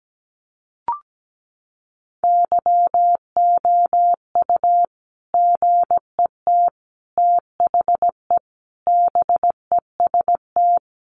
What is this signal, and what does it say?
Morse code, heard as a keyed tone, about 17 wpm, 700 Hz. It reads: YOU GET THE BEST
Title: Sentences from Top 1200 Words 17wpm